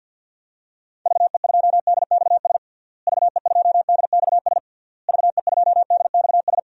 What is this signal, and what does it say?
Morse code, heard as a keyed tone, about 52 wpm, 700 Hz. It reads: VE2DXS VE2DXS VE2DXS
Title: DXCC-100 Call Signs - 6 Characters - Repeated 3x - 50wpm